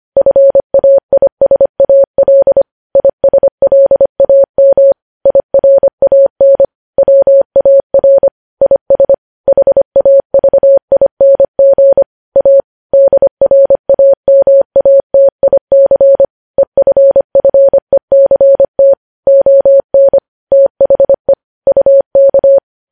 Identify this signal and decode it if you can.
Morse code, heard as a keyed tone, about 25 wpm, 550 Hz. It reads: FAISAL ISLAM IRAN WAR IS HAVING A DRAMATIC EFFECT ON THE UK
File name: podcast / News headlines-1x-25x25